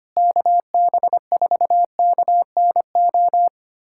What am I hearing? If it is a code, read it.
Morse code, heard as a keyed tone, about 25 wpm, 700 Hz. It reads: KB4KNO
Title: USA Callsign DE Callsign - 6 Characters - 25wpm